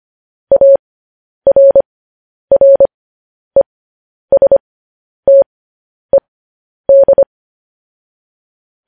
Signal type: Morse code